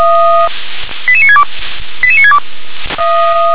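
Five-tone selective call (ZVEI), heard as quick runs of five tones, tones of about 70 ms. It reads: 88962 88962
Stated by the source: ZVEI five-tone selective call, code 88694, received by SDR.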